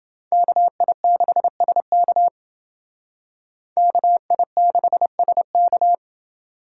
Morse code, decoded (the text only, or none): KI6SK KI6SK